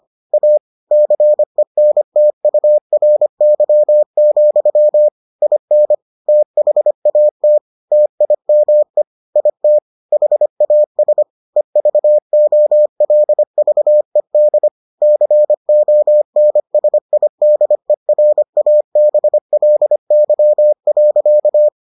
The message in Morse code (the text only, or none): A CENTURY, IN THAT TIME IT HAS EVOLVED CONSIDERABLY.